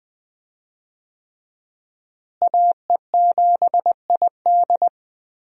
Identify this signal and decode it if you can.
Morse code, heard as a keyed tone, about 20 wpm, 700 Hz. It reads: AE7ID
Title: USA Call Signs - 5 Characters - Repeated 3x - 20wpm